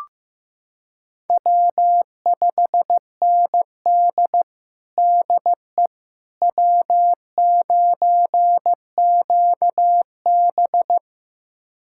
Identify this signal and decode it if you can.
Morse code, heard as a keyed tone, about 16 wpm, 700 Hz. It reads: W5ND DE W9QB
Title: USA Callsign DE Callsign - 4 Characters - 15wpm